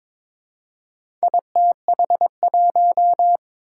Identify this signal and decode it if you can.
Morse code, heard as a keyed tone, about 22 wpm, 700 Hz. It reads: ITH1
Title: Four Challenging Letter-Number Combinations 22wpm